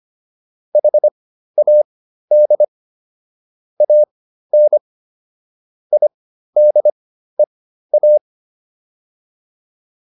Morse code, heard as a keyed tone, about 25 wpm, spaced as for 13 wpm, 600 Hz. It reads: HAD AN IDEA